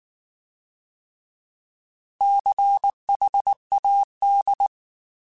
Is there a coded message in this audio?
Yes: Morse code from 2.2 to 4.7 s, heard as a keyed tone, about 19 wpm, 800 Hz. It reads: CHAD